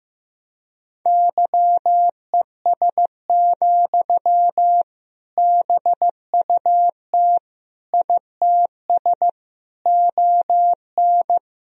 Morse code, heard as a keyed tone, about 15 wpm, 700 Hz. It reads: YES, BUT ITS ON
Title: Sentences from Top 1000 Words - Review Part 1 - 15wpm